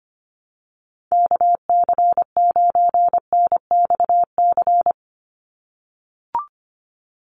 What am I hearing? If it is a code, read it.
Morse code, heard as a keyed tone, about 25 wpm, 700 Hz. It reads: KC9NXC
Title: USA Call Signs 25wpm